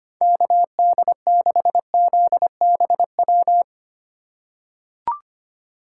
Morse code, heard as a keyed tone, about 25 wpm, 700 Hz. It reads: KD6ZBW
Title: USA Call Signs - 6 Characters - 4x Character Spacing - 25wpm